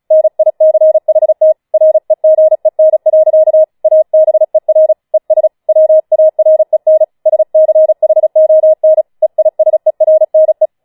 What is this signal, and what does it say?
Morse code, heard as a keyed tone, about 36 wpm, 630 Hz. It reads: NICHT REGEN. ABER ES WAREN SCHON EISERNE
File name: Schneewittchen Teil6 36wpm